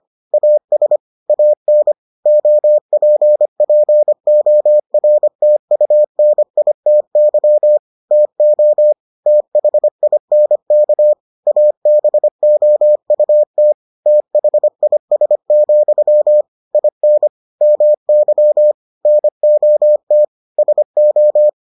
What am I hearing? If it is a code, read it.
Morse code, heard as a keyed tone, about 25 wpm, 600 Hz. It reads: AS AN OPPORTUNITY TO THINK ABOUT THIS, IN MY NOT SO